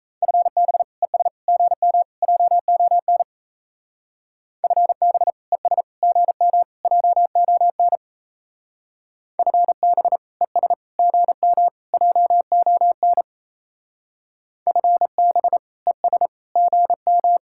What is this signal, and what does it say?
Morse code, heard as a keyed tone, about 37 wpm, 700 Hz. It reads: FB ES GM JON FB ES GM JON FB ES GM JON FB ES GM